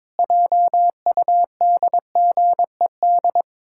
Morse code, heard as a keyed tone, about 22 wpm, 700 Hz. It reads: JUDGED
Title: Pangram Challenge v2 - 22wpm